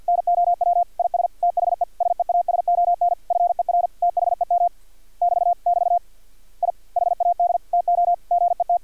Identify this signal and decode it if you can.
Morse code, heard as a keyed tone, about 50 wpm, 700 Hz. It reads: NOW IS THE SEASON FER THEM <BT> <BT> I HAD TO GET
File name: archived / CRSnet